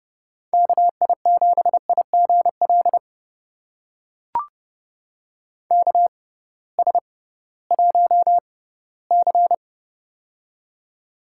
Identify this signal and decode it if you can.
Morse code, heard as a keyed tone, about 30 wpm, spaced as for 18 wpm, 700 Hz. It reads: KI7IGL KS1C
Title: USA Call Signs - 6x Character Spacing - 30wpm